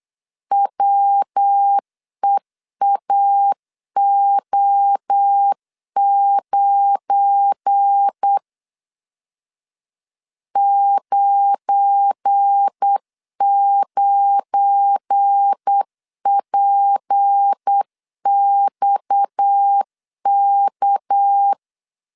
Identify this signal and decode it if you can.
Morse code, heard as a keyed tone, about 8 wpm, 800 Hz. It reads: WEAO9 99PXK
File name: LEZIONI / 7 GF9Z / 7 GF9 8wpm 3545